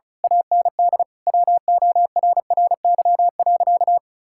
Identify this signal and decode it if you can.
Morse code, heard as a keyed tone, about 36 wpm, 700 Hz. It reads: AND WORRY.
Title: Pangram Challenge v2 - 35wpm